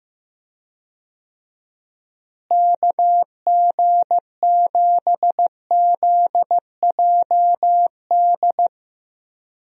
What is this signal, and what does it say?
Morse code, heard as a keyed tone, about 15 wpm, 700 Hz. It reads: KG7ZJD
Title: USA Call Signs - 6 Characters - Repeated 3x - 15wpm